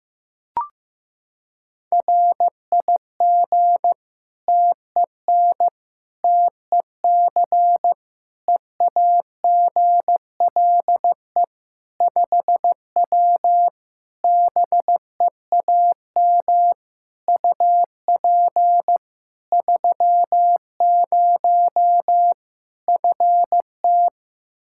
Morse code, heard as a keyed tone, about 16 wpm, 700 Hz. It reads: RIG TEN TEC EAGLE 5W BEAM UP 30 FT
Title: Practice QSO - Exchange 2 - Info - 15wpm